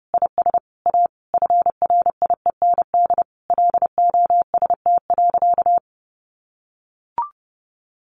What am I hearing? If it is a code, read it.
Morse code, heard as a keyed tone, about 30 wpm, 700 Hz. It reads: IS A FRIEND LOST.
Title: Sentences from Top 1400 Words 30wpm